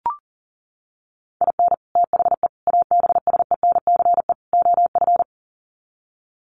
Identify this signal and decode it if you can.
Morse code, heard as a keyed tone, about 40 wpm, 700 Hz. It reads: IN THE ABSENCE OF
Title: Sets of 4 Words 40wpm